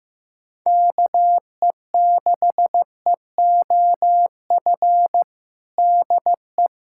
Morse code, heard as a keyed tone, about 15 wpm, 700 Hz. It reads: KE6EOF DE
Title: USA Callsign DE Callsign - 6 Characters - 15wpm